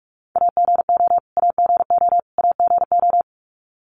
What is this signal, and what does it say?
Morse code, heard as a keyed tone, about 44 wpm, 700 Hz. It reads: AGO AGO AGO
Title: Top 500 Words Repeated 45wpm